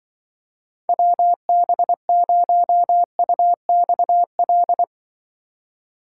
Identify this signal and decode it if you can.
Morse code, heard as a keyed tone, about 24 wpm, 700 Hz. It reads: WB0UXL
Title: Speed Racing - USA 2x3 Call Signs 20wpm